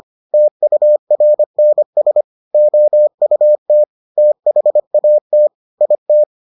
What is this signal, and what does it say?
Morse code, heard as a keyed tone, about 25 wpm, 600 Hz. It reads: TURNS OUT THAT IT